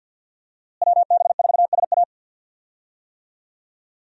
Morse code, heard as a keyed tone, about 47 wpm, 700 Hz. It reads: WB4SA
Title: USA Call Signs - 5 Characters - Repeated 2x - 50wpm